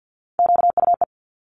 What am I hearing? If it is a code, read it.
Morse code, heard as a keyed tone, about 50 wpm, 700 Hz. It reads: QUE
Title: Multi-letter Phonemes Quick Fire 50wpm